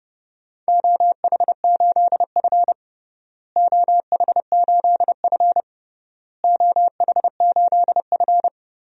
Morse code, heard as a keyed tone, about 30 wpm, 700 Hz. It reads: OH8F OH8F OH8F
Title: DXCC-100 Call Signs - 4 Characters - Repeated 3x - 30wpm